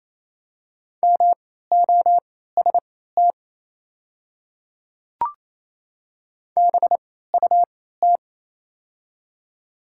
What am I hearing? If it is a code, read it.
Morse code, heard as a keyed tone, about 27 wpm, spaced as for 16 wpm, 700 Hz. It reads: MOST BUT
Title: 3x Character Spacing - Top 100 Words - 28wpm